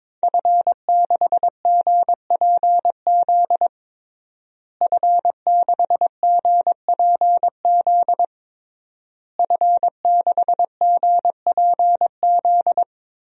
Morse code, heard as a keyed tone, about 22 wpm, 700 Hz. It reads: F6GPZ F6GPZ F6GPZ
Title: DXCC-100 Call Signs - 5 Characters - Repeated 3x - 22wpm